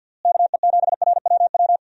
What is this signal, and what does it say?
Morse code, heard as a keyed tone, about 52 wpm, 700 Hz. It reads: KE7RWW
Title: USA Call Signs - 6 Characters - Repeated 2x - 50wpm